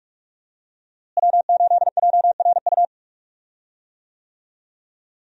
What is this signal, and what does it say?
Morse code, heard as a keyed tone, about 46 wpm, 700 Hz. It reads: W8JRU
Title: USA Call Signs - 5 Characters - Repeated 3x - 45wpm